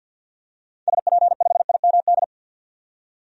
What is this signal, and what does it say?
Morse code, heard as a keyed tone, about 51 wpm, 700 Hz. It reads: SP5IMD
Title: DXCC-100 Call Signs - 6 Characters - 50wpm